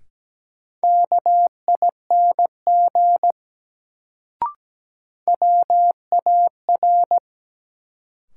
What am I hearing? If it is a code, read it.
Morse code, heard as a keyed tone, about 17 wpm, 700 Hz. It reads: KING WAR
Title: Sentences from Top 400 Words - New Word Drill - 17wpm